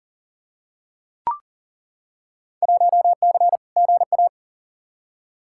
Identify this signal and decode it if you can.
Morse code, heard as a keyed tone, about 40 wpm, 700 Hz. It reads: 1C GA
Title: Field Day Exchanges - 40wpm